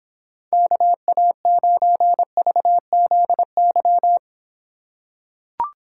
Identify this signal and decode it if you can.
Morse code, heard as a keyed tone, about 25 wpm, 700 Hz. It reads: KA9VZY